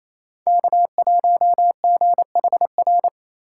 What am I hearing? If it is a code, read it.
Morse code, heard as a keyed tone, about 28 wpm, 700 Hz. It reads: K1GHR
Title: USA Call Signs - 5 Characters - 4x Character Spacing - 28wpm